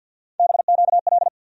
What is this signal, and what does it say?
Morse code, heard as a keyed tone, about 48 wpm, 700 Hz. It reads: BQL
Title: USA Call Sign Postfixes - 3 Letters 50wpm